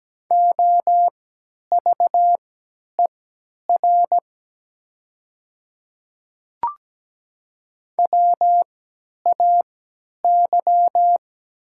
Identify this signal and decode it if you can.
Morse code, heard as a keyed tone, about 17 wpm, spaced as for 10 wpm, 700 Hz. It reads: OVER WAY
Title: 3x Character Spacing - Top 100 Words - 17wpm